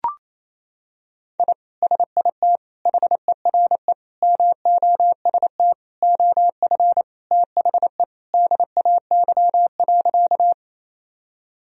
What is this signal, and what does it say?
Morse code, heard as a keyed tone, about 28 wpm, 700 Hz. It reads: I SIT HERE MOST OF THE DAY.
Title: Sentences from Top 1000 Words - Review Part 1 - 28wpm